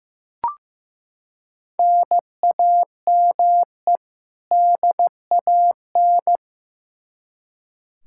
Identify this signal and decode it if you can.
Morse code, heard as a keyed tone, about 15 wpm, 700 Hz. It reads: NAME DAN
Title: Practice QSO - Exchange 1 - Beginning - 15wpm